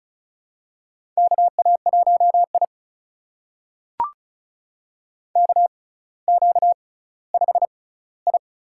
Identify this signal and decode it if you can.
Morse code, heard as a keyed tone, about 36 wpm, spaced as for 18 wpm, 700 Hz. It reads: KA1I KQ5I
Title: USA Call Signs - 4 Characters - 6x Character Spacing - 35wpm